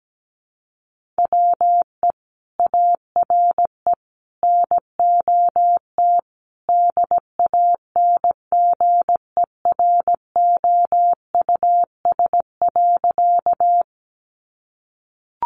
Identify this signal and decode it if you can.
Morse code, heard as a keyed tone, about 17 wpm, 700 Hz. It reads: WE ARE NOT DANGEROUS.
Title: Sentences from Top 1200 Words 17wpm